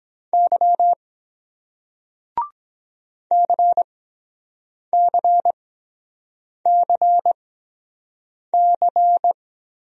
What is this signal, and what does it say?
Morse code, heard as a keyed tone, about 23 wpm, spaced as for 7 wpm, 700 Hz. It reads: Y CCCC